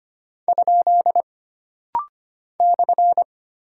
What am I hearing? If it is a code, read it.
Morse code, heard as a keyed tone, about 25 wpm, 700 Hz. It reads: ? /
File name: Single Punctuation-Prosign - Rapid-Fire - 25wpm